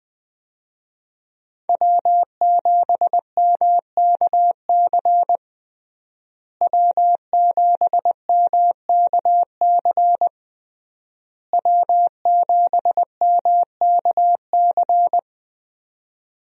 Morse code, heard as a keyed tone, about 21 wpm, 700 Hz. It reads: W7MKC W7MKC W7MKC
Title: USA Call Signs - 5 Characters - Repeated 3x - 20wpm